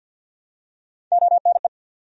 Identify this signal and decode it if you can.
Morse code, heard as a keyed tone, about 54 wpm, 700 Hz. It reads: ONE
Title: Top 100 Words 50wpm